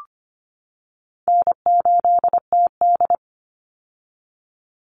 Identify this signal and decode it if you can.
Morse code, heard as a keyed tone, about 25 wpm, 700 Hz. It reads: N8TD
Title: Call Signs - 25wpm